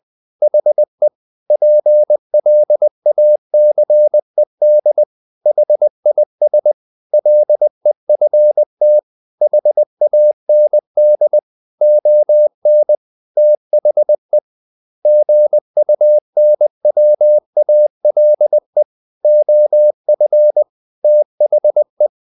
Morse code, heard as a keyed tone, about 20 wpm, 600 Hz. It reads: HE PLACED HIS LEFT HAND ON THE GUNWALE OF THE
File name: the old man and the sea part 2 20wpm